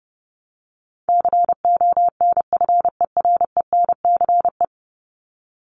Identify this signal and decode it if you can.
Morse code, heard as a keyed tone, about 30 wpm, 700 Hz. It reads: CONFERENCE
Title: Top 1751-2000 Words 30wpm